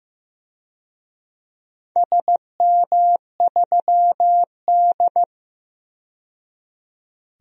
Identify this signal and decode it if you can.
Morse code, heard as a keyed tone, about 15 wpm, 700 Hz. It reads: SM3D
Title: DXCC-100 Call Signs - 4 Characters - 15wpm